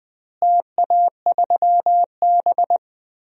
Morse code, heard as a keyed tone, about 21 wpm, 700 Hz. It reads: TA3B
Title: DXCC-100 Call Signs - 4 Characters - Repeated 2x - 20wpm